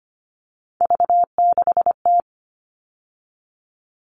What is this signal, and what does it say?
Morse code, heard as a keyed tone, about 25 wpm, 700 Hz. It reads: V6T
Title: All Cut Numbers - 3-Digit 25wpm